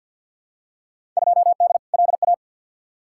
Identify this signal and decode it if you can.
Morse code, heard as a keyed tone, about 50 wpm, 700 Hz. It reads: 2D LA
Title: Field Day Exchanges - 50wpm